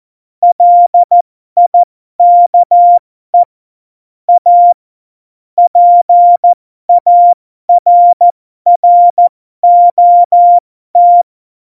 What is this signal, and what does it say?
Morse code, heard as a keyed tone, about 14 wpm, spaced as for 12 wpm, 700 Hz. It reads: LIKE A PARROT
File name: Morsekurs / Witze